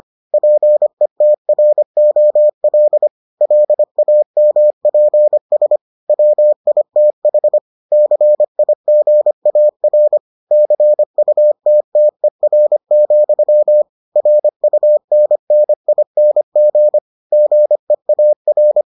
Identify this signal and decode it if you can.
Morse code, heard as a keyed tone, about 25 wpm, 600 Hz. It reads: PETROL LAMPS WITH CIGAR CUTTER, RUNNING GEAR